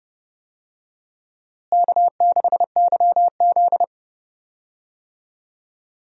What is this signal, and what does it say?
Morse code, heard as a keyed tone, about 31 wpm, 700 Hz. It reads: K6YZ
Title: USA Call Signs - 4 Characters - Repeated 3x - 30wpm